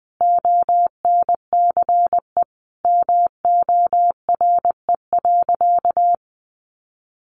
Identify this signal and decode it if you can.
Morse code, heard as a keyed tone, about 19 wpm, 700 Hz. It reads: ONCE MORE.
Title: Sentences from Top 1100 Words 20wpm